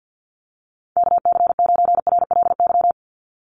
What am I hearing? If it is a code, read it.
Morse code, heard as a keyed tone, about 50 wpm, 700 Hz. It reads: KC9RLY
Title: USA Call Signs 50wpm